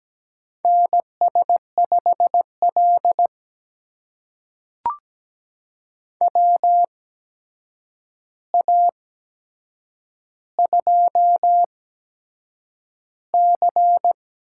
Morse code, heard as a keyed tone, about 17 wpm, spaced as for 7 wpm, 700 Hz. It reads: NS5L WA2C
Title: USA Call Signs - 4 Characters - 8x Character Spacing - 17wpm